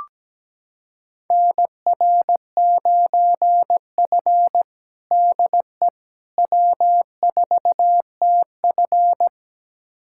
Morse code, heard as a keyed tone, about 17 wpm, 700 Hz. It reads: NR9F DE W4TF
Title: USA Callsign DE Callsign - 4 Characters - 17wpm